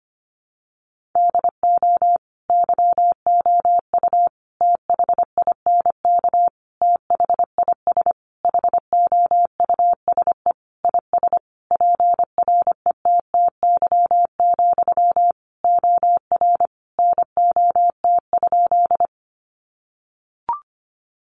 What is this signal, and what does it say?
Morse code, heard as a keyed tone, about 25 wpm, 700 Hz. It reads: DO YOU THINK THIS HOUSE IS PRETTY, OR NOT?
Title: Sentences from Top 1000 Words 25wpm